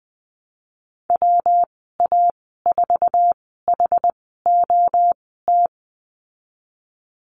Morse code, heard as a keyed tone, about 20 wpm, spaced as for 14 wpm, 700 Hz. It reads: WA4HOT
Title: 140 Lesson 23g - Letter f - Call Signs 20wpm